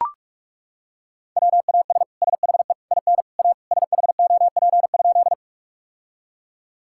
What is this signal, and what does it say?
Morse code, heard as a keyed tone, about 43 wpm, 700 Hz. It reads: WAS SHE IN A SHOP?
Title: Sentences from Top 1600 Words 45wpm